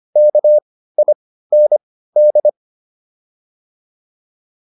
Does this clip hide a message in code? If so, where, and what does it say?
Yes: Morse code from 0.2 to 2.5 s, heard as a keyed tone, about 25 wpm, spaced as for 15 wpm, 600 Hz. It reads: KIND